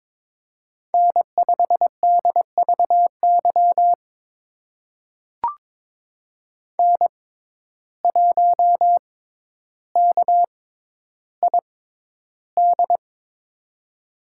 Morse code, heard as a keyed tone, about 22 wpm, spaced as for 11 wpm, 700 Hz. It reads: N5DVY N1KID
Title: USA Call Signs - 5 Characters - 6x Character Spacing - 22wpm